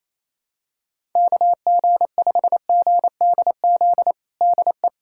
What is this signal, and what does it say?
Morse code, heard as a keyed tone, about 28 wpm, 700 Hz. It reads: KG5GDZ DE
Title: USA Callsign DE Callsign - 6 Characters - 28wpm